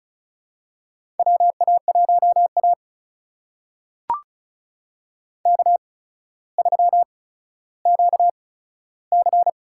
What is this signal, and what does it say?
Morse code, heard as a keyed tone, about 36 wpm, spaced as for 15 wpm, 700 Hz. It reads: WA1A K3QC
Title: USA Call Signs - 4 Characters - 8x Character Spacing - 35wpm